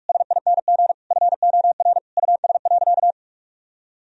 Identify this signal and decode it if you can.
Morse code, heard as a keyed tone, about 46 wpm, 700 Hz. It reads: SING FOR US.
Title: Sentences from Top 1000 Words - Review Part 2 - 45wpm